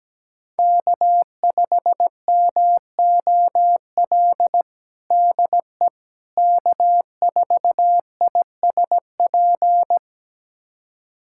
Morse code, heard as a keyed tone, about 17 wpm, 700 Hz. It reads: K5MOL DE K4ISP
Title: USA Callsign DE Callsign - 5 Characters - 17wpm